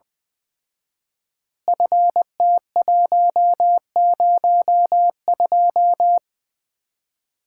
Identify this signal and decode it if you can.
Morse code, heard as a keyed tone, about 20 wpm, 700 Hz. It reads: FT102
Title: Common QSO Phrases - 4x Word-Spacing - 20wpm